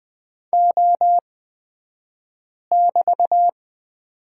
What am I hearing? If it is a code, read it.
Morse code, heard as a keyed tone, about 20 wpm, spaced as for 5 wpm, 700 Hz. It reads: O<BT>
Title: 223 Lesson 36d - BT - Two Characters 20wpm